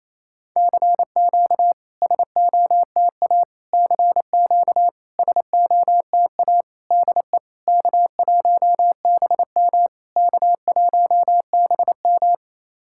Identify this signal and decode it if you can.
Morse code, heard as a keyed tone, about 28 wpm, 690 Hz. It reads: CQ SOTA CQ SOTA DE K1BM K1BM
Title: SOTA QSO Practice 28wpm